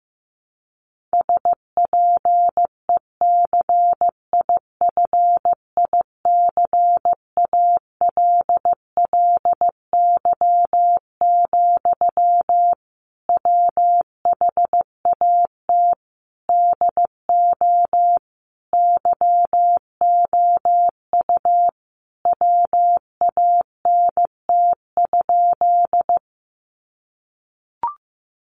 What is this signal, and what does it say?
Morse code, heard as a keyed tone, about 15 wpm, 700 Hz. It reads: SPECIFICALLY, WHAT DO YOU WANT?
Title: Sentences from Top 1100 Words 15wpm